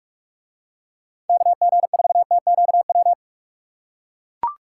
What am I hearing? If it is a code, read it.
Morse code, heard as a keyed tone, about 44 wpm, 700 Hz. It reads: KG4TQW